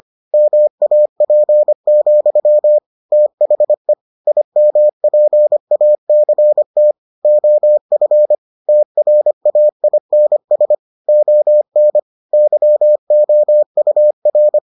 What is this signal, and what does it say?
Morse code, heard as a keyed tone, about 25 wpm, 600 Hz. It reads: MAP, THE IMPACT OF TRAINS ON YOUR